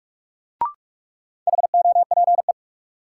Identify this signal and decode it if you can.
Morse code, heard as a keyed tone, about 46 wpm, 700 Hz. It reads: HOPE
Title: Sentences from Top 1000 Words - New Word Drill - 45wpm